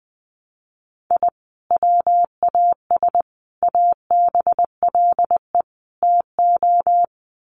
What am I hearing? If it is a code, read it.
Morse code, heard as a keyed tone, about 20 wpm, 700 Hz. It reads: I WAS ABLE TO
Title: Sets of 4 Words 20wpm